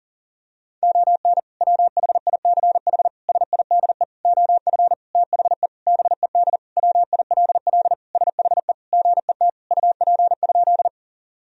Morse code, heard as a keyed tone, about 40 wpm, 700 Hz. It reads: ON WHICH SIDE OF THE BED WILL SHE GET UP?
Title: Sentences from Top 1000 Words - Review Part 1 - 40wpm